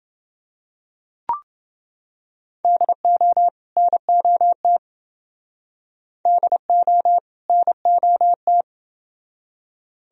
Morse code, heard as a keyed tone, about 27 wpm, 700 Hz. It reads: DO NOT DO NOT